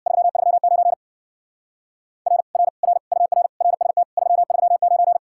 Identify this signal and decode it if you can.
Morse code, heard as a keyed tone, about 62 wpm, 700 Hz. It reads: 339 R R R UR RST 339
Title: Speed-Racing Common QSO Phrases Encore - 50wpm